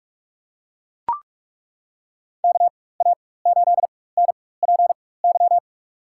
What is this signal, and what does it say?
Morse code, heard as a keyed tone, about 42 wpm, spaced as for 20 wpm, 700 Hz. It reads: KA8NPY
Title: USA Call Signs - 4x Character Spacing - 45wpm